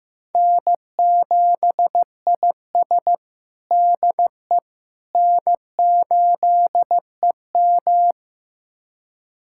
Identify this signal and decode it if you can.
Morse code, heard as a keyed tone, about 15 wpm, 700 Hz. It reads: N7IS DE N8EM
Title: USA Callsign DE Callsign - 4 Characters - 15wpm